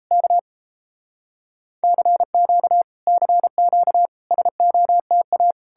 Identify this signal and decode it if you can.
Morse code, heard as a keyed tone, about 38 wpm, 700 Hz. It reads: K CQ CQ SOTA